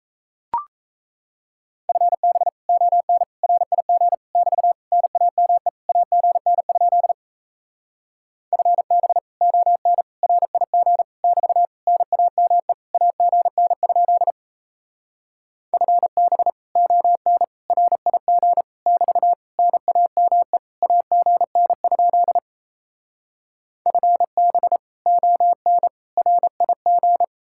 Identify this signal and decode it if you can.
Morse code, heard as a keyed tone, about 37 wpm, 700 Hz. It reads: FB ON RIG <BT> NAME AGN? FB ON RIG <BT> NAME AGN? FB ON RIG <BT> NAME AGN? FB ON RIG